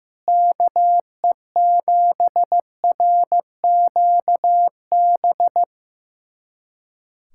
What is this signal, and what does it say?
Morse code, heard as a keyed tone, about 15 wpm, 700 Hz. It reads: KE7RQB
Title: USA Callsign DE Callsign - 6 Characters - 15wpm